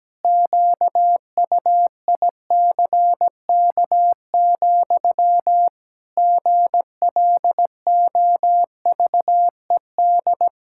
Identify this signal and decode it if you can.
Morse code, heard as a keyed tone, about 17 wpm, 700 Hz. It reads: QUICK, GLOVED
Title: Pangram Challenge v2 - 17wpm